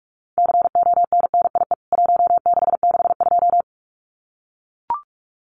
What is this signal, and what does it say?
Morse code, heard as a keyed tone, about 44 wpm, 700 Hz. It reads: CONNIE 1662
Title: CWT Exchange 45wpm